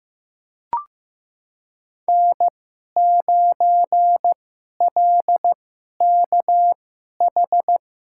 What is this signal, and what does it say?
Morse code, heard as a keyed tone, about 15 wpm, spaced as for 11 wpm, 700 Hz. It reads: N9LKH
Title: USA Call Signs - 5 Characters - 2x Character Spacing - 15wpm